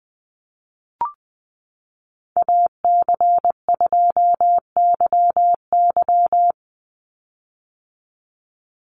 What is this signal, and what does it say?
Morse code, heard as a keyed tone, about 20 wpm, 700 Hz. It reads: AC2YY
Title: USA Call Signs 20wpm